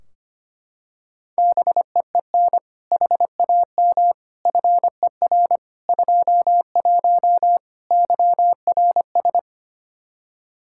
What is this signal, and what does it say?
Morse code, heard as a keyed tone, about 25 wpm, 700 Hz. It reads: BEEN HAM FER 21 YRS
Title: Practice QSO - Exchange 3 - More Info - 25wpm